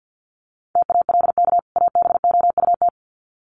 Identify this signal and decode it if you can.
Morse code, heard as a keyed tone, about 48 wpm, 700 Hz. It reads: TALK ABOUT
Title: Sets of 2 Words 50wpm